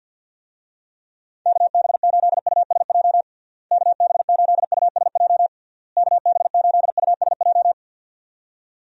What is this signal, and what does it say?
Morse code, heard as a keyed tone, about 52 wpm, 700 Hz. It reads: KB8USJ KB8USJ KB8USJ
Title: USA Call Signs - 6 Characters - Repeated 3x - 50wpm